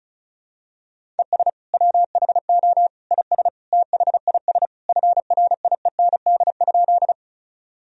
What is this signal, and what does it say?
Morse code, heard as a keyed tone, about 35 wpm, 700 Hz. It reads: ES WHO IS THIS FRIEND?
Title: Sentences from Top 300 Words and Abbreviations 35wpm